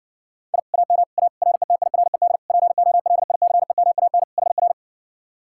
Morse code, heard as a keyed tone, about 70 wpm, 700 Hz. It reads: I AM A RETIRED POLICEMAN HR